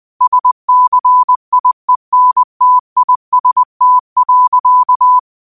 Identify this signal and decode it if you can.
Morse code, heard as a keyed tone, about 20 wpm, 1000 Hz. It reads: SCIENTIST.